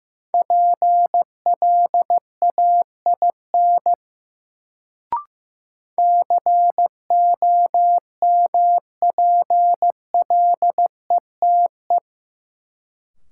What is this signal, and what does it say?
Morse code, heard as a keyed tone, about 16 wpm, 700 Hz. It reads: PLAIN COMPLETE
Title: Sentences from Top 400 Words - New Word Drill - 15wpm